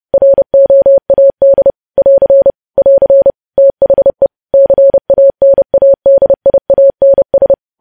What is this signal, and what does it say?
Morse code, heard as a keyed tone, about 30 wpm, 550 Hz. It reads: ROAD <AR> <AR> THE CANADIANS
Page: news headlines in morse code at 30 wpm